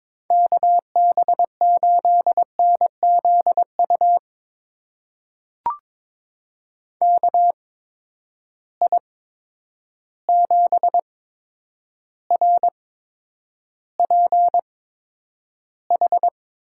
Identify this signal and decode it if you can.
Morse code, heard as a keyed tone, about 22 wpm, spaced as for 9 wpm, 700 Hz. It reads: KB8NZU KI7RPH
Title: USA Call Signs - 6 Characters - 8x Character Spacing - 22wpm